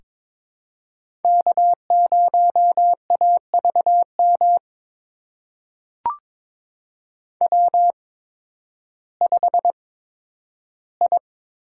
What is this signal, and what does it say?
Morse code, heard as a keyed tone, about 22 wpm, spaced as for 12 wpm, 700 Hz. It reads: K0AVM W5I
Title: USA Call Signs - 5 Characters - 8x Character Spacing - 22wpm